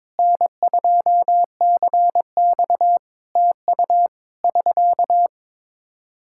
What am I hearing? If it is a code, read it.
Morse code, heard as a keyed tone, about 22 wpm, 700 Hz. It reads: N2CX TU <SK>
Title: Practice QSO - Exchange 4 - Ending - 22wpm